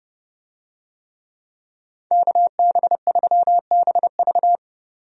Morse code, heard as a keyed tone, about 30 wpm, 700 Hz. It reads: KB3BV
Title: USA Call Signs - 5 Characters - 30wpm